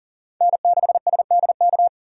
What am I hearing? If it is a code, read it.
Morse code, heard as a keyed tone, about 40 wpm, 700 Hz. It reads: N6SDK
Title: USA Call Signs - 3x Character Spacing - 40wpm